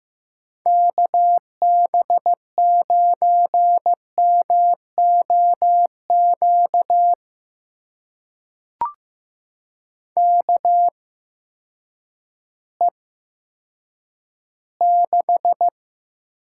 Morse code, heard as a keyed tone, about 15 wpm, spaced as for 9 wpm, 700 Hz. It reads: KB9MOQ KE6
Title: USA Call Signs - 6 Characters - 8x Character Spacing - 15wpm